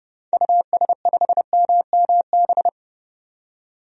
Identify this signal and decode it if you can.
Morse code, heard as a keyed tone, about 32 wpm, 700 Hz. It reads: US5MMB
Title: DXCC-100 Call Signs - 6 Characters - 30wpm